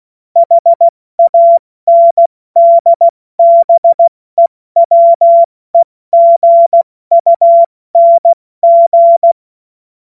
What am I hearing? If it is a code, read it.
Morse code, heard as a keyed tone, about 16 wpm, spaced as for 14 wpm, 680 Hz. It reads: HANDBEWEGUNG
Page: cw-witze